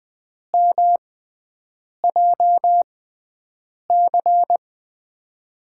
Morse code, heard as a keyed tone, about 20 wpm, spaced as for 7 wpm, 700 Hz. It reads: MJC
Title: USA Call Signs - 5 Characters - 6x Character Spacing - 20wpm